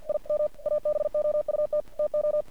Morse code, heard as a keyed tone, about 50 wpm, 600 Hz. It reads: IM ABOUT TO
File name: archived / CRSnet